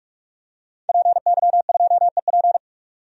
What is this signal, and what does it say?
Morse code, heard as a keyed tone, about 46 wpm, 700 Hz. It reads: PY2EP
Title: DXCC-100 Call Signs - 5 Characters - 45wpm